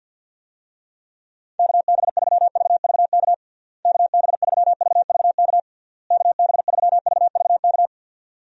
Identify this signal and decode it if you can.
Morse code, heard as a keyed tone, about 49 wpm, 700 Hz. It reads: KB3VVK KB3VVK KB3VVK
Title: USA Call Signs - 6 Characters - Repeated 3x - 50wpm